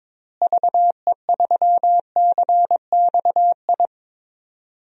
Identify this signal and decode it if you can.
Morse code, heard as a keyed tone, about 22 wpm, 700 Hz. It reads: VE3CXI
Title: DXCC-100 Call Signs - 6 Characters - Repeated 3x - 22wpm